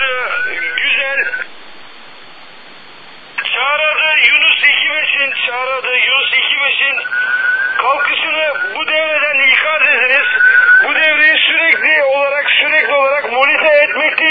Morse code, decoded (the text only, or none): none